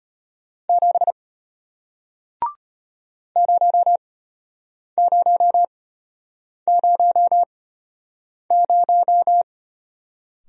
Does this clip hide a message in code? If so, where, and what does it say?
Yes: Morse code from 0.7 to 9.4 s, heard as a keyed tone, about 33 wpm, spaced as for 8 wpm, 700 Hz. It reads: 7 0000